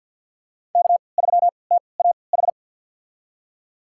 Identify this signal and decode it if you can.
Morse code, heard as a keyed tone, about 52 wpm, spaced as for 29 wpm, 700 Hz. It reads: K3TAH
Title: USA Call Signs - 5 Characters - 3x Character Spacing - 50wpm